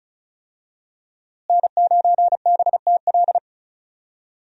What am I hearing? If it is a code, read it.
Morse code, heard as a keyed tone, about 34 wpm, 700 Hz. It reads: N9BTL
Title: USA Call Signs - 5 Characters - Repeated 3x - 35wpm